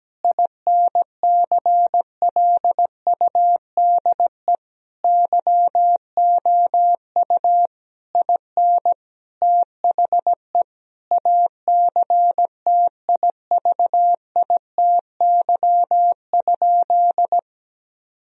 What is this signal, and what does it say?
Morse code, heard as a keyed tone, about 17 wpm, 700 Hz. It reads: INCLUDE YOU IN THE ACTIVITY?
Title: Sentences from Top 1000 Words - Review Part 4 - 17wpm